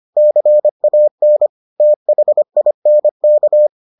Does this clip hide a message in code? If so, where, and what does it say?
Yes: Morse code from 0.2 to 3.7 s, heard as a keyed tone, about 25 wpm, 600 Hz. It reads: CAN THINK